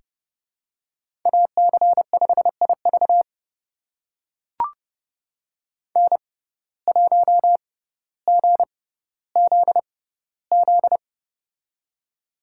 Morse code, heard as a keyed tone, about 30 wpm, spaced as for 15 wpm, 700 Hz. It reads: AC5IV N1GZZ
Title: USA Call Signs - 5 Characters - 6x Character Spacing - 30wpm